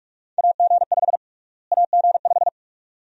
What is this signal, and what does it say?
Morse code, heard as a keyed tone, about 46 wpm, 700 Hz. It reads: AG5 AG5
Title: USA Call Sign Prefixes - Repeated 2x - 45wpm